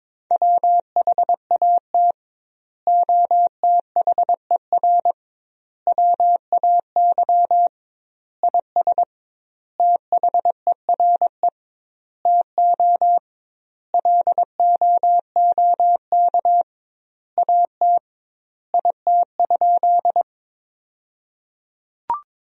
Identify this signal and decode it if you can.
Morse code, heard as a keyed tone, about 22 wpm, 700 Hz. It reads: WHAT OTHER WAY IS THERE TO LOOK AT IT?
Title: 2x Word-Spacing - Sentences from Top 100 Words - 22wpm